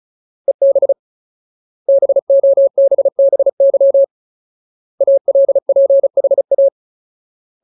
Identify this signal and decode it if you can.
Morse code, heard as a keyed tone, about 34 wpm, 550 Hz. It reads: EB BOBBY ALPHA
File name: Ultimate Word List / 35WPM Ultimate